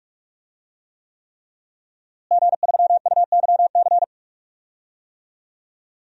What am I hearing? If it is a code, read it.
Morse code, heard as a keyed tone, about 43 wpm, 700 Hz. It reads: G3UYC
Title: DXCC-100 Call Signs - 5 Characters - Repeated 3x - 45wpm